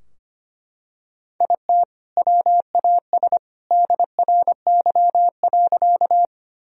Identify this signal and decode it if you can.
Morse code, heard as a keyed tone, about 25 wpm, 700 Hz. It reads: IT WAS DRY.
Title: Sentences from Top 1000 Words - Review Part 3 - 25wpm